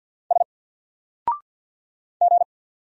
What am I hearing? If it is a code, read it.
Morse code, heard as a keyed tone, about 48 wpm, 700 Hz. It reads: S G
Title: Single Character (Letter:Number:Punctuation:Prosign) Alt 50wpm